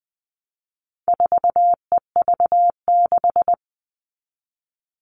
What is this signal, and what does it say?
Morse code, heard as a keyed tone, about 20 wpm, 700 Hz. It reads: 4EV6
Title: All Cut Numbers - 4-Digit 20wpm